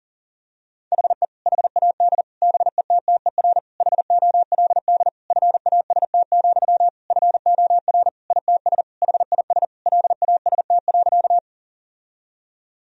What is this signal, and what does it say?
Morse code, heard as a keyed tone, about 40 wpm, 700 Hz. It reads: HE HAD BETTER HOLD FAST, FOR ITS HIS LAST.
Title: Sentences from Top 1000 Words - Review Part 2 - 40wpm